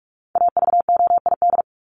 Morse code, heard as a keyed tone, about 44 wpm, 700 Hz. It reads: AVOID
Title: Top 751-1000 Words 45wpm